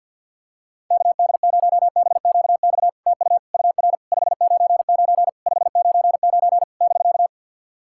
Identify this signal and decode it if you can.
Morse code, heard as a keyed tone, about 50 wpm, 700 Hz. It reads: KD0BQX TU UR 599 599 <BK>
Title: SOTA QSO Practice 50wpm